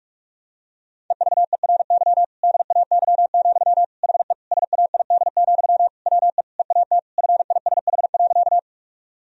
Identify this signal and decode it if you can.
Morse code, heard as a keyed tone, about 46 wpm, 700 Hz. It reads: EVERY DAY, HE SAID, WE EAT FISH.
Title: Sentences from Top 1000 Words - Review Part 1 - 45wpm